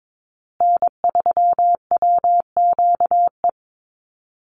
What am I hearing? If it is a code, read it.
Morse code, heard as a keyed tone, about 22 wpm, 700 Hz. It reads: N3WQE
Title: USA Call Signs Repeated 2x 22wpm